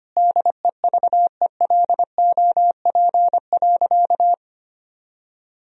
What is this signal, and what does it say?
Morse code, heard as a keyed tone, about 25 wpm, 700 Hz. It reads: DEVELOP.
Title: Sentences from Top 1000 Words - Review Part 2 - 25wpm